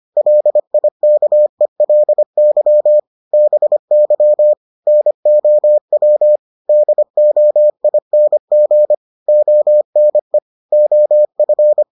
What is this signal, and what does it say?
Morse code, heard as a keyed tone, about 25 wpm, 600 Hz. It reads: LIKELY BY NOW DOING ONE OF